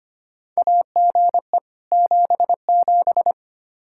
Morse code, heard as a keyed tone, about 25 wpm, 700 Hz. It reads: AGE 77
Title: Practice QSO - Exchange 3 - More Info - 25wpm